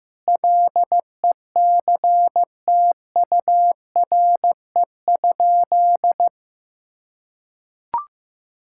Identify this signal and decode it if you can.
Morse code, heard as a keyed tone, about 15 wpm, 700 Hz. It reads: LECTURE?
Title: Sentences from Top 1600 Words 15wpm